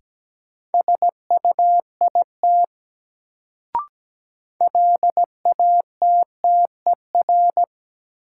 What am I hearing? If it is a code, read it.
Morse code, heard as a keyed tone, about 17 wpm, 700 Hz. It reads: SUIT LATTER
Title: Sentences from Top 1600 Words - New Word Drill Alt - 17wpm